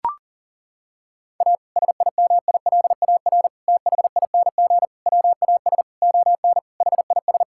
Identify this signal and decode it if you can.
Morse code, heard as a keyed tone, about 37 wpm, 700 Hz. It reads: A SIMILAR THING WAS ON HIS
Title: Sentences from Top 1000 Words - Review Part 5 - 40wpm